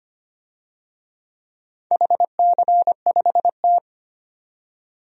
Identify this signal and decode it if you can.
Morse code, heard as a keyed tone, about 25 wpm, 700 Hz. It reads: HC5T
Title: DXCC-100 Call Signs - 4 Characters - 25wpm